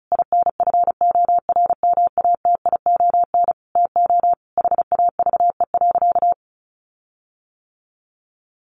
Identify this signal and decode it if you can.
Morse code, heard as a keyed tone, about 34 wpm, 700 Hz. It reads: INFORMATION TO HAVE.
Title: Sentences from Top 1400 Words 35wpm